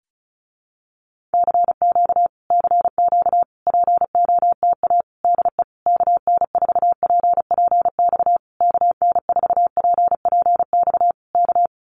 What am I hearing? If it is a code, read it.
Morse code, heard as a keyed tone, about 34 wpm, 700 Hz. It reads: CQ CQ POTA DE KN4PPX KN4PPX K
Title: Common QSO Phrases Encore 35wpm